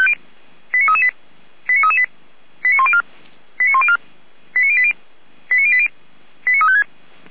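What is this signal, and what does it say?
Five-tone selective call (ZVEI), heard as quick runs of five tones, tones of about 70 ms. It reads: 89338 89338 89104 89104 89080 89080 89356
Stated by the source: Five-tone ZVEI selective call received on an SDR.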